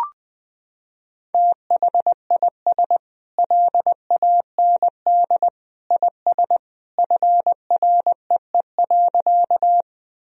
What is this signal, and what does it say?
Morse code, heard as a keyed tone, about 21 wpm, 700 Hz. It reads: THIS LAND IS FREE.
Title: Sentences from Top 1000 Words - Review Part 1 - 20wpm